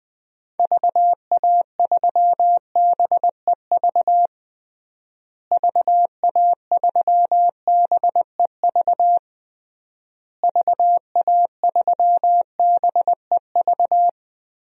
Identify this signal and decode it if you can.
Morse code, heard as a keyed tone, about 19 wpm, 700 Hz. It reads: VA3BEV VA3BEV VA3BEV
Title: DXCC-100 Call Signs - 6 Characters - Repeated 3x - 20wpm